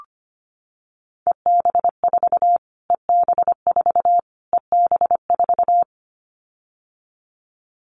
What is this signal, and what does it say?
Morse code, heard as a keyed tone, about 25 wpm, 700 Hz. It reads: EB4 EB4 EB4
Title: Callsign Prefixes for Top 25 DXCC - Repeated 3x - 25wpm